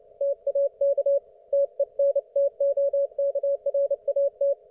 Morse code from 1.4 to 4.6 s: TENTOKRAT